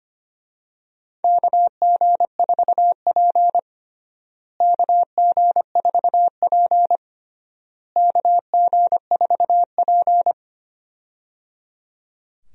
Morse code, heard as a keyed tone, about 25 wpm, 700 Hz. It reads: KG4P KG4P KG4P
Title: USA Call Signs - 4 Characters - Repeated 3x - 25wpm